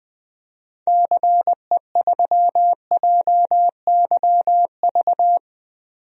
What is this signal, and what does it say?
Morse code, heard as a keyed tone, about 20 wpm, 700 Hz. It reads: CE3JYV
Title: DXCC-100 Call Signs - 6 Characters - 20wpm